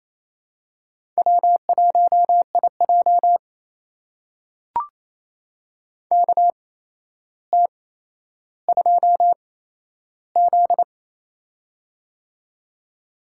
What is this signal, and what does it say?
Morse code, heard as a keyed tone, about 28 wpm, spaced as for 12 wpm, 700 Hz. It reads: W1IJ KT2Z
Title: USA Call Signs - 4 Characters - 8x Character Spacing - 28wpm